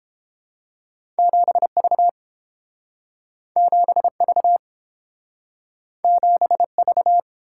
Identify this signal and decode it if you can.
Morse code, heard as a keyed tone, about 29 wpm, 700 Hz. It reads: 7V 7V 7V